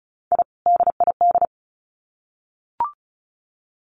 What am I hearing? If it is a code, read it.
Morse code, heard as a keyed tone, about 34 wpm, 700 Hz. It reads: I DID
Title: Sets of 2 Words 35wpm